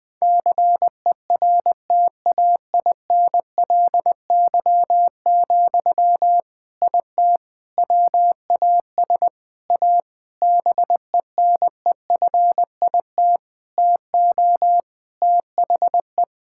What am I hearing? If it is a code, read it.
Morse code, heard as a keyed tone, about 20 wpm, 700 Hz. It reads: CERTAINLY, IT WAS A BENEFIT TO THE
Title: Sentences from Top 1000 Words - Review Part 6 - 20wpm